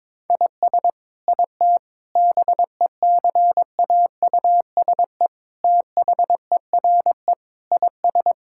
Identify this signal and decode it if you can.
Morse code, heard as a keyed tone, about 22 wpm, 700 Hz. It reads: IS IT BECAUSE THERE IS
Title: Sentences from Top 1000 Words - Review Part 5 - 22wpm